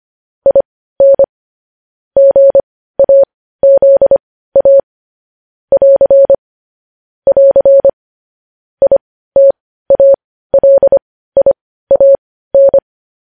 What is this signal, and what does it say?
Morse code, heard as a keyed tone, about 25 wpm, spaced as for 15 wpm, 550 Hz. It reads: IN GAZA <AR> <AR> ITALIAN